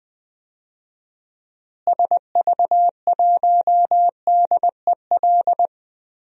Morse code, heard as a keyed tone, about 21 wpm, 700 Hz. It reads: SV1DEL
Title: DXCC-100 Call Signs - 6 Characters - Repeated 2x - 20wpm